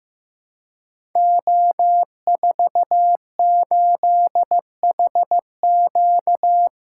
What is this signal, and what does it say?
Morse code, heard as a keyed tone, about 15 wpm, 700 Hz. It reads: O48HQ
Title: Code Groups - 5 Letter:Numbers - 15wpm